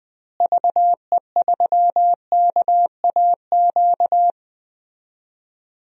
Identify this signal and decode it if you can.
Morse code, heard as a keyed tone, about 20 wpm, 700 Hz. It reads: VE3KAQ
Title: DXCC-100 Call Signs - 6 Characters - Repeated 2x - 20wpm